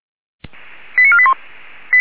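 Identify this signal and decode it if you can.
Five-tone selective call (ZVEI), heard as quick runs of five tones, tones of about 70 ms. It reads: 89481
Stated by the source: ZVEI five-tone selective call received by SDR.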